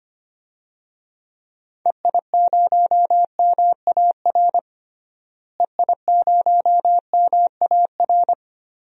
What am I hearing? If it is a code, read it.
Morse code, heard as a keyed tone, about 25 wpm, 700 Hz. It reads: EI0MAR EI0MAR
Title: DXCC-100 Call Signs - 6 Characters - Repeated 3x - 25wpm